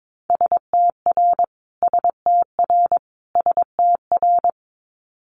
Morse code, heard as a keyed tone, about 22 wpm, 700 Hz. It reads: STR STR STR
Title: Multi-letter Phonemes Repeated 3x 22wpm